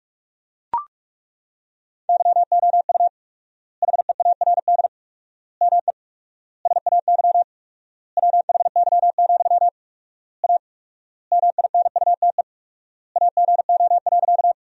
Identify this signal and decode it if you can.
Morse code, heard as a keyed tone, about 42 wpm, 700 Hz. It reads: YOU HEARD ME SAY WHY, A MINUTE AGO.
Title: 4x Word-Spacing - Sentences from Top 500 Words - 45wpm